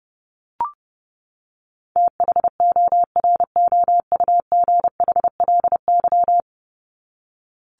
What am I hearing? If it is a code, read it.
Morse code, heard as a keyed tone, about 30 wpm, 700 Hz. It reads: THOROUGHLY
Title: LY Suffix - Top 100 Words with Base Words 30wpm